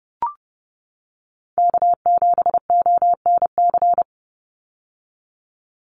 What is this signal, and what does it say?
Morse code, heard as a keyed tone, about 29 wpm, 700 Hz. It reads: K7ONC
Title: USA Call Signs 30wpm